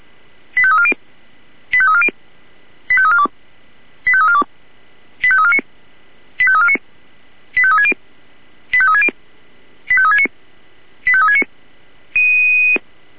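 Five-tone selective call (ZVEI), heard as quick runs of five tones, tones of about 70 ms. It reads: 85340 85340 85342 85342 85349 85349 85360 85360 85369 85369
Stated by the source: ZVEI five-tone selective call received on an SDR.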